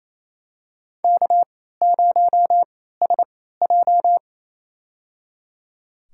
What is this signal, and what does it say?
Morse code, heard as a keyed tone, about 28 wpm, spaced as for 16 wpm, 700 Hz. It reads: K0SJ
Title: USA Call Signs - 4 Characters - 3x Character Spacing - 28wpm